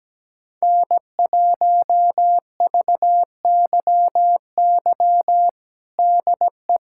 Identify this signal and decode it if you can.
Morse code, heard as a keyed tone, about 17 wpm, 700 Hz. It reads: N1VYY DE
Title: USA Callsign DE Callsign - 5 Characters - 17wpm